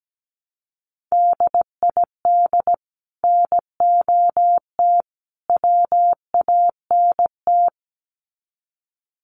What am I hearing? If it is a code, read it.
Morse code, heard as a keyed tone, about 17 wpm, 700 Hz. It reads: DID NOT WANT
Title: Sets of 3 Words 17wpm